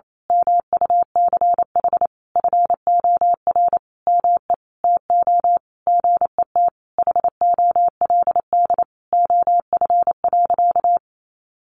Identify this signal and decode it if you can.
Morse code, heard as a keyed tone, about 28 wpm, 700 Hz. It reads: MUCH FOR ME TO GET HOLD OF.
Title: A Sound Of Thunder 28wpm